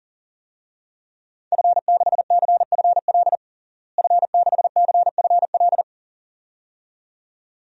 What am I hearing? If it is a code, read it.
Morse code, heard as a keyed tone, about 37 wpm, 700 Hz. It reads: F6CFL F6CFL
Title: DXCC-100 Call Signs - 5 Characters - Repeated 2x - 40wpm